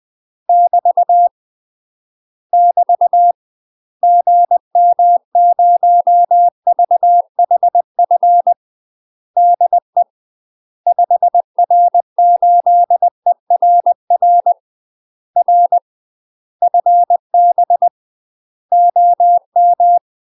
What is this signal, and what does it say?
Morse code, heard as a keyed tone, about 20 wpm, 700 Hz. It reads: <BT> <BT> GM0VHF DE 5R8ERR R FB OM
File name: UK MASTER QSOs TRAINING / 20WPM / UK MASTER QSOs 0062